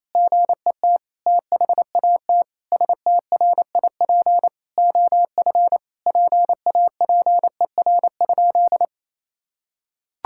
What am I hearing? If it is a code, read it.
Morse code, heard as a keyed tone, about 28 wpm, 700 Hz. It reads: GET THAT STRIP OF PAPER?
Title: Sentences from Top 1600 Words 28wpm